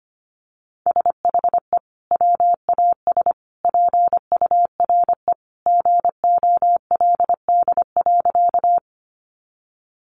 Morse code, heard as a keyed tone, about 25 wpm, 700 Hz. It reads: SHE WAS PURE GOLD.
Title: Sentences from Top 1500 Words 25wpm